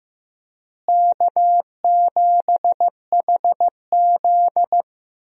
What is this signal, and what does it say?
Morse code, heard as a keyed tone, about 15 wpm, 700 Hz. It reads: K7HZ
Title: USA Call Signs - 4 Characters - Repeated 3x - 15wpm